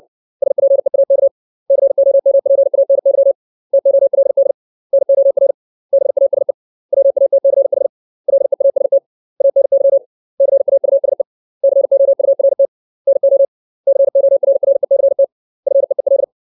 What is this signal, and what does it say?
Morse code, heard as a keyed tone, about 60 wpm, 550 Hz. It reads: SPEAK COMPANY TOLD NOR BASE WATCH LEAST ANY CAUSE COURT NO CORRECT FEEL
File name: 632 words / 632 Words 60WPM